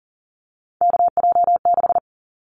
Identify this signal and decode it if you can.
Morse code, heard as a keyed tone, about 40 wpm, 700 Hz. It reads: KJ6
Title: Callsign Prefixes for Top 25 DXCC - 40wpm